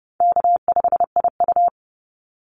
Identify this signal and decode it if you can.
Morse code, heard as a keyed tone, about 30 wpm, 700 Hz. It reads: K5IU
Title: Easy Call Signs Repeated 2x - 30wpm